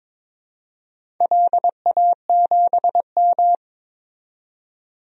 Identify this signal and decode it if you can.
Morse code, heard as a keyed tone, about 22 wpm, 700 Hz. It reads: LA7M
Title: DXCC-100 Call Signs - 4 Characters - 22wpm